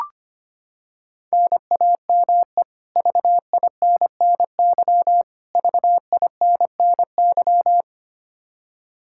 Morse code, heard as a keyed tone, about 25 wpm, 700 Hz. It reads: NAME VINNY VINNY
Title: K1USN SST QSO Practice 25wpm